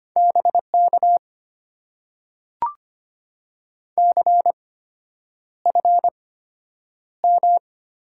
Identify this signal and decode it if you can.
Morse code, heard as a keyed tone, about 25 wpm, spaced as for 9 wpm, 700 Hz. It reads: BK CFM
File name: Top 90 QSO Elements - 8x Character Spacing - 25wpm